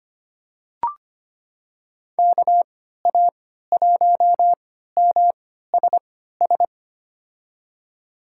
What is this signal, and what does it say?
Morse code, heard as a keyed tone, about 25 wpm, spaced as for 14 wpm, 700 Hz. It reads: KA1MSS
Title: USA Call Signs - 6 Characters - 3x Character Spacing - 25wpm